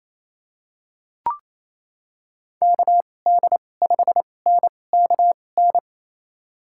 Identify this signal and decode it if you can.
Morse code, heard as a keyed tone, about 28 wpm, spaced as for 20 wpm, 700 Hz. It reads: KD5NKN
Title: USA Call Signs - 6 Characters - 2x Character Spacing - 28wpm